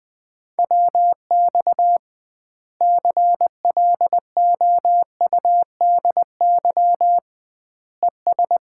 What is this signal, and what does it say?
Morse code, heard as a keyed tone, about 20 wpm, 700 Hz. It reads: WX CLOUDY ES
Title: Common QSO Phrases - 2x Word-Spacing - 20wpm